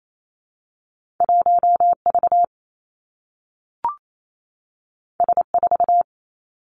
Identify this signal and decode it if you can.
Morse code, heard as a keyed tone, about 28 wpm, 700 Hz. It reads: 1 V S 4
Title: Two Challenging Letter-Number Combinations 28wpm